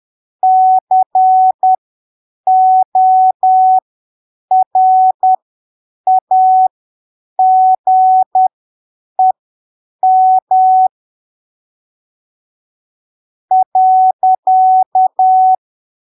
Morse code, heard as a keyed tone, about 10 wpm, spaced as for 6 wpm, 750 Hz. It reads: CORAGEM.
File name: arquivos / texto / texto 19 10ppm